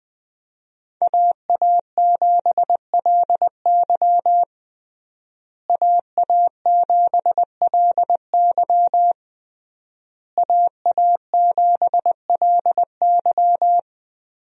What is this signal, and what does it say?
Morse code, heard as a keyed tone, about 21 wpm, 700 Hz. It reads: AA7LY AA7LY AA7LY
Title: USA Call Signs - 5 Characters - Repeated 3x - 20wpm